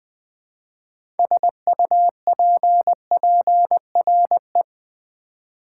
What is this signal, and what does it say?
Morse code, heard as a keyed tone, about 20 wpm, 700 Hz. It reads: SUPPRE
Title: Build A Word - 2 Letters At A Time - 8-Letter Words - 20wpm